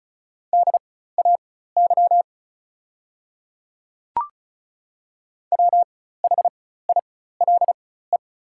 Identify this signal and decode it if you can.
Morse code, heard as a keyed tone, about 36 wpm, spaced as for 16 wpm, 700 Hz. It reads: DAY WHILE
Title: 4x Character Spacing - Top 300 Words - 35wpm